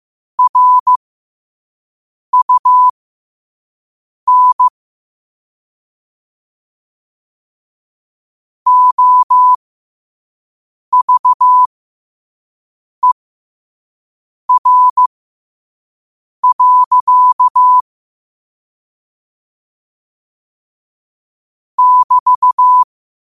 Morse code, heard as a keyed tone, about 15 wpm, spaced as for 5 wpm, 1000 Hz. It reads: RUN OVER. <BT>
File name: qotdfiles / 5wpm